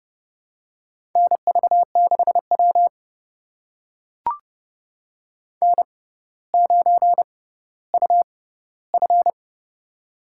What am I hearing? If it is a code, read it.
Morse code, heard as a keyed tone, about 30 wpm, spaced as for 15 wpm, 700 Hz. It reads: NV6W N9UF
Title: USA Call Signs - 4 Characters - 6x Character Spacing - 30wpm